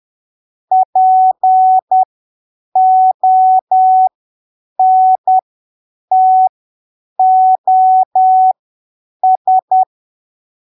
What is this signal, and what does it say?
Morse code, heard as a keyed tone, about 10 wpm, spaced as for 7 wpm, 750 Hz. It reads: PONTOS
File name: arquivos / texto / texto 04 10ppm